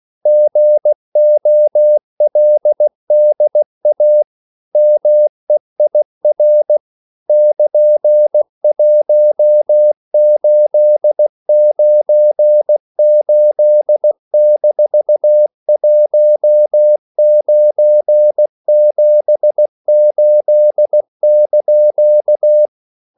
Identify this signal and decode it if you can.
Morse code, heard as a keyed tone, about 16 wpm, 600 Hz. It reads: GOLDA MEIR <KN>1898-1978)
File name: CW / Practice / EnglishQuotations0116